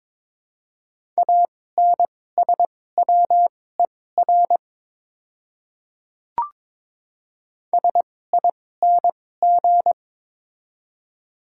Morse code, heard as a keyed tone, about 23 wpm, spaced as for 16 wpm, 700 Hz. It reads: ANSWER SING
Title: 2x Character Spacing - Top 500 Words - 22wpm